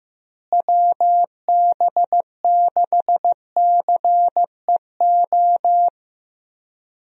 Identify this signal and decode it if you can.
Morse code, heard as a keyed tone, about 15 wpm, 700 Hz. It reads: WB6CEO
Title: USA Call Signs - 6 Characters - Repeated 3x - 15wpm